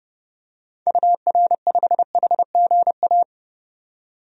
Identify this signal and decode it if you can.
Morse code, heard as a keyed tone, about 30 wpm, 700 Hz. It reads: UR5HGA
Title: DXCC-100 Call Signs - 6 Characters - 30wpm